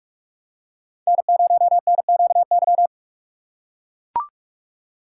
Morse code, heard as a keyed tone, about 41 wpm, 700 Hz. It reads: N0NQY
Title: USA Call Signs - 5 Characters - 2x Character Spacing - 45wpm